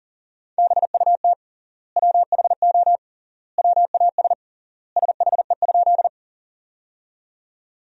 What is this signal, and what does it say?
Morse code, heard as a keyed tone, about 40 wpm, 700 Hz. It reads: BUT WHO WAS SHE?
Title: 3x Word-Spacing - Sentences from Top 100 Words - 40wpm